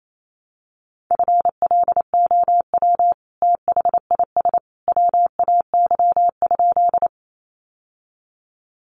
Morse code, heard as a keyed tone, about 28 wpm, 700 Hz. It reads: FLOW THIS WAY?
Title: Sentences from Top 1200 Words 28wpm